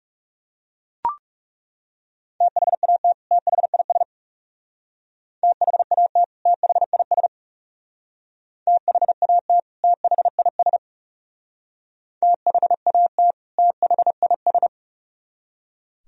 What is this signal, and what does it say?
Morse code, heard as a keyed tone, about 39 wpm, 700 Hz. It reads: THAT THIS THAT THIS THAT THIS THAT THIS